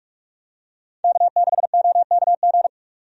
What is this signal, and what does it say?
Morse code, heard as a keyed tone, about 47 wpm, 700 Hz. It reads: K6OKG
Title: USA Call Signs - 5 Characters - 45wpm